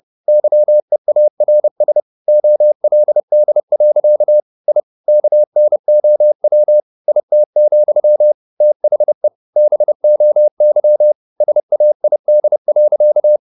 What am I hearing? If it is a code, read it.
Morse code, heard as a keyed tone, about 30 wpm, 600 Hz. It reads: YEARS OLD. I KNOW IT, THE BOY SAID.
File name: the old man and the sea / the old man and the sea 1 30wpm